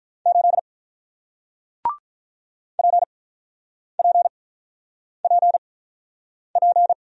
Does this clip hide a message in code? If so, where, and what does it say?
Yes: Morse code from 0.3 to 6.9 s, heard as a keyed tone, about 46 wpm, spaced as for 9 wpm, 700 Hz. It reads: 8 PPPP